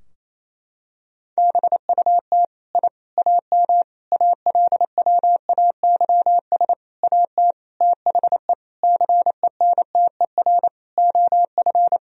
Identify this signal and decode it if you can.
Morse code, heard as a keyed tone, about 27 wpm, 700 Hz. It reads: BUT I AM ALWAYS AT THE CENTER OF
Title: Sentences from Top 1000 Words - Review Part 2 - 28wpm